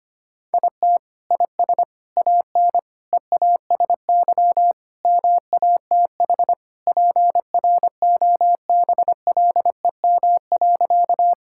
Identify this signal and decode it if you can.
Morse code, heard as a keyed tone, about 25 wpm, 700 Hz. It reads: IT IS AN EASY MATH PROBLEM.
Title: Sentences from Top 1000 Words - Review Part 6 - 25wpm